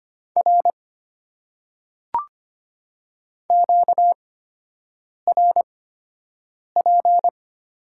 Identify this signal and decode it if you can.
Morse code, heard as a keyed tone, about 25 wpm, spaced as for 7 wpm, 700 Hz. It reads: R QRP
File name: Top 90 QSO Elements - 8x Character Spacing - 25wpm